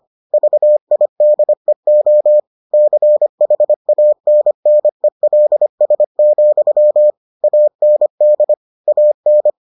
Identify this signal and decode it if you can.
Morse code, heard as a keyed tone, about 25 wpm, 600 Hz. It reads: VIDEO CHANNELS, AND AN